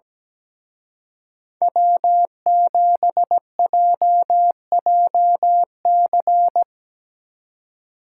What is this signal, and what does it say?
Morse code, heard as a keyed tone, about 18 wpm, 700 Hz. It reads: W7JJC
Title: Speed Racing - USA 1x3 Call Signs 17wpm